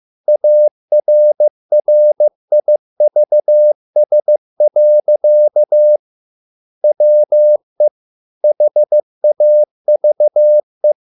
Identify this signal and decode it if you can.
Morse code, heard as a keyed tone, about 15 wpm, 600 Hz. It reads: ARRIVS. WE HAVE